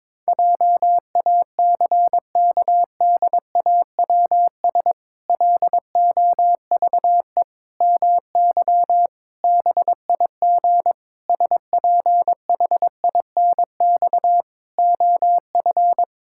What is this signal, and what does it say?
Morse code, heard as a keyed tone, about 22 wpm, 700 Hz. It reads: JACKDAWS LOVE MY BIG SPHINX OF
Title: Pangram Challenge v2 - 22wpm